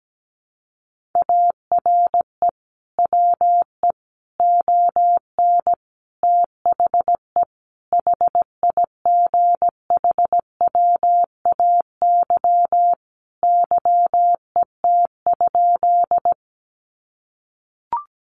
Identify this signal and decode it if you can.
Morse code, heard as a keyed tone, about 17 wpm, 700 Hz. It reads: ARE WE ON THE HIGHWAY YET?
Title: Sentences from Top 1400 Words 17wpm